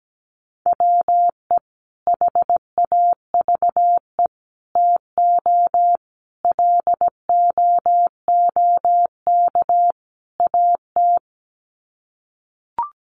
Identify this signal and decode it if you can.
Morse code, heard as a keyed tone, about 17 wpm, 700 Hz. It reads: WE HAVE TO LOOK AT
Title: Sets of 5 Words Encore2 17wpm